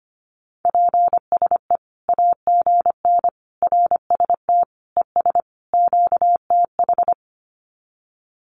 Morse code, heard as a keyed tone, about 25 wpm, 700 Hz. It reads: PSE AGN RST ES QTH
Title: Common QSO Phrases 25wpm